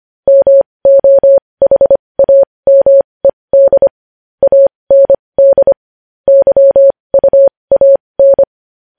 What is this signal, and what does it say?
Morse code, heard as a keyed tone, about 25 wpm, spaced as for 20 wpm, 550 Hz. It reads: MOHAMED AND YUAN
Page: news headlines in morse code at 20 wpm